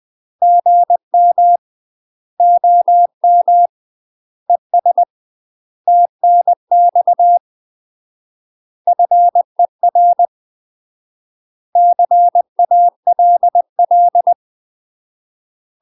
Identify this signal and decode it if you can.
Morse code, heard as a keyed tone, about 20 wpm, 700 Hz. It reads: GM OM ES TNX FER CALL
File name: UK MASTER QSOs TRAINING / 20WPM / UK MASTER QSOs 0037